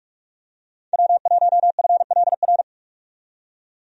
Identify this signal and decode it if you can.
Morse code, heard as a keyed tone, about 46 wpm, 700 Hz. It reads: W1FLR
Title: USA Call Signs - 5 Characters - 45wpm